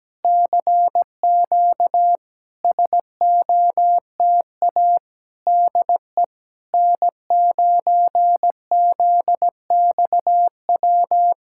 Morse code, heard as a keyed tone, about 17 wpm, 700 Hz. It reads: CQ SOTA DE N9ZXW
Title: SOTA QSO Practice 17wpm